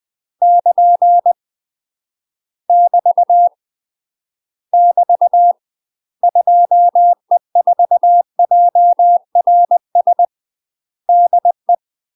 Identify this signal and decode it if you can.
Morse code, heard as a keyed tone, about 20 wpm, 700 Hz. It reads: <KN> <BT> <BT> 2E4JRS DE
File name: UK MASTER QSOs TRAINING / 20WPM / UK MASTER QSOs 0045